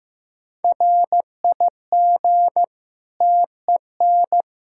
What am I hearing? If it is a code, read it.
Morse code, heard as a keyed tone, about 15 wpm, 700 Hz. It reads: RIG TEN
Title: K1USN SST QSO Practice 15wpm